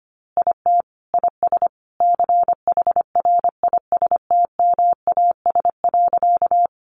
Morse code, heard as a keyed tone, about 25 wpm, 700 Hz. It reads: IT IS CHRISTMAS.
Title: Easy Phrases 25wpm